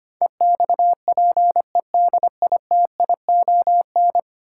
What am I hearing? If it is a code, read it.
Morse code, heard as a keyed tone, about 25 wpm, 700 Hz. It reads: EXPEDITION
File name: Ten Letter Words - 25wpm